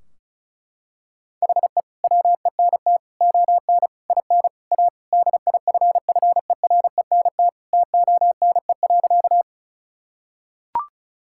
Morse code, heard as a keyed tone, about 36 wpm, 700 Hz. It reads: HE WENT ON IN A DIFFERENT TONE.
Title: Sentences from Top 1600 Words 35wpm